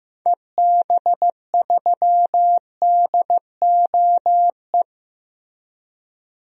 Morse code, heard as a keyed tone, about 15 wpm, 700 Hz. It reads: EB3DOE
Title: DXCC-100 Call Signs - 6 Characters - Repeated 3x - 15wpm